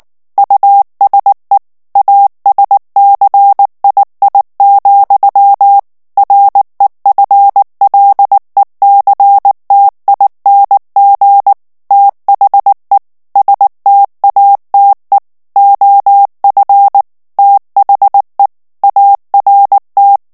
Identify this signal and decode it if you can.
Morse code, heard as a keyed tone, about 19 wpm, 800 Hz. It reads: USE ASCII, REFLECTING THE STATE OF THE ART
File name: ANGLAIS texte clair 3 vitesse 18 mots minute